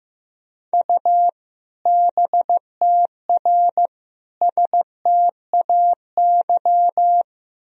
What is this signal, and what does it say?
Morse code, heard as a keyed tone, about 15 wpm, 700 Hz. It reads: U BTR STAY
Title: Sentences from Top 500 Words and Abbreviations 15wpm